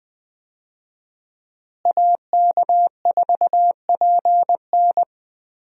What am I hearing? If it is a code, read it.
Morse code, heard as a keyed tone, about 20 wpm, 700 Hz. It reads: AK4PN
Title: USA Call Signs - 5 Characters - Repeated 2x - 20wpm